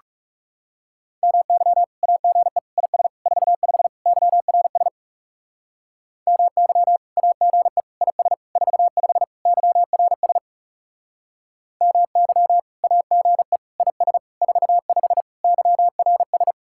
Morse code, heard as a keyed tone, about 40 wpm, 700 Hz. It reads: MY AGE IS 45 YRS MY AGE IS 45 YRS MY AGE IS 45 YRS